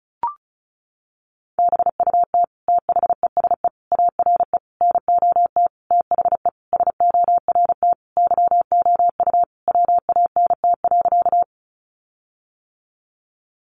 Morse code, heard as a keyed tone, about 34 wpm, 700 Hz. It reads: BUT THESE ARE NOT THE SORT YOU WANT.
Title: Sentences from Top 1300 Words 35wpm